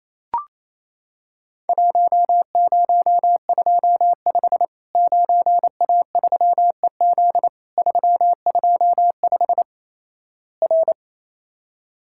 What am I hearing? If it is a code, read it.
Morse code, heard as a keyed tone, about 27 wpm, 700 Hz. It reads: 1025 9A3EZ 325 R
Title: Worked All Europe DX Contest - QTC Exchange - 28wpm